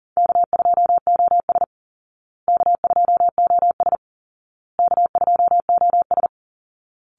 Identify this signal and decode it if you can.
Morse code, heard as a keyed tone, about 37 wpm, 700 Hz. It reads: K2OS K2OS K2OS
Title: Easy Call Signs Repeated 3x - 40wpm